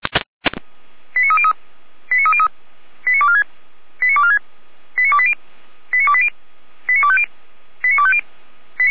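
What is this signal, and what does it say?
Five-tone selective call (ZVEI), heard as quick runs of five tones, tones of about 70 ms. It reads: 89303 89303 89246 89246 89280 89280 89250 89250
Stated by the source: Five-tone ZVEI selective call received by SDR, from the feuerwehr-orlamuende.org fire brigade audio archive.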